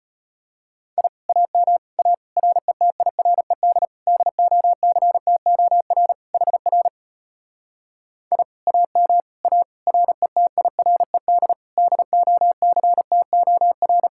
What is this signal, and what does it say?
Morse code, heard as a keyed tone, about 33 wpm, 700 Hz. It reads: I AM A RETIRED DOCTOR HR I AM A RETIRED DOCTOR